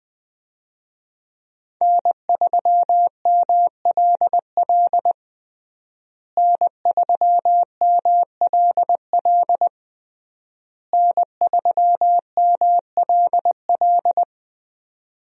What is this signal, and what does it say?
Morse code, heard as a keyed tone, about 21 wpm, 700 Hz. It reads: N3MLL N3MLL N3MLL
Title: USA Call Signs - 5 Characters - Repeated 3x - 20wpm